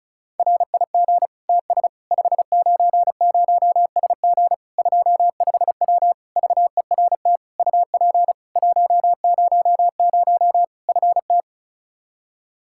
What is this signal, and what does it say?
Morse code, heard as a keyed tone, about 36 wpm, 700 Hz. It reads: RIG TS 590SG 25W VERT UP 100 FT
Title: Practice QSO - Exchange 2 - Info - 35wpm